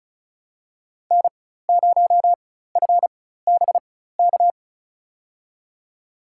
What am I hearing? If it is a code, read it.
Morse code, heard as a keyed tone, about 34 wpm, spaced as for 16 wpm, 700 Hz. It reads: N0FBK
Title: USA Call Signs - 5 Characters - 4x Character Spacing - 35wpm